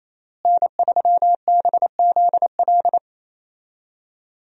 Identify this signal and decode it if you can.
Morse code, heard as a keyed tone, about 28 wpm, 700 Hz. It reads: N3BZL
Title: USA Call Signs - 5 Characters - Repeated 3x - 28wpm